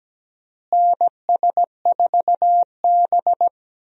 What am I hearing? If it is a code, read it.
Morse code, heard as a keyed tone, about 17 wpm, 700 Hz. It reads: NS4B
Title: USA Call Signs - 4 Characters - Repeated 3x - 17wpm